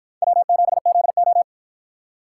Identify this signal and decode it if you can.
Morse code, heard as a keyed tone, about 53 wpm, 700 Hz. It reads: W7ZO